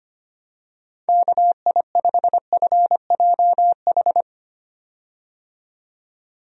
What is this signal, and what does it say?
Morse code, heard as a keyed tone, about 25 wpm, 700 Hz. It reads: KI5FJH
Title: USA Call Signs - 6 Characters - 25wpm